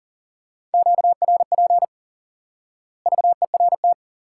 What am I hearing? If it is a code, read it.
Morse code, heard as a keyed tone, about 39 wpm, 700 Hz. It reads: QRP VERT
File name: IQE Course 3 - Lesson 89 - YAGI - Two Elements - 40wpm